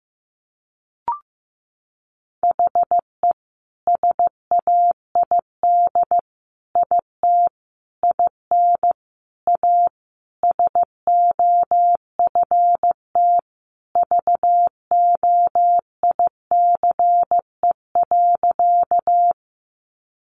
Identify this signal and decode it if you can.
Morse code, heard as a keyed tone, about 15 wpm, 700 Hz. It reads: HE SAID IT IN A SOFT VOICE.
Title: Sentences from Top 1500 Words 15wpm